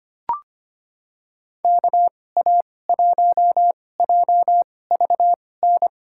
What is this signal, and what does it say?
Morse code, heard as a keyed tone, about 25 wpm, spaced as for 18 wpm, 700 Hz. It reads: KA1JVN
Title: USA Call Signs - 6 Characters - 2x Character Spacing - 25wpm